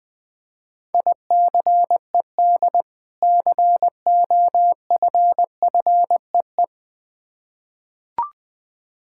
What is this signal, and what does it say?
Morse code, heard as a keyed tone, about 20 wpm, 690 Hz. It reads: ICED COFFEE
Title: Difficult Words with Hints - 20wpm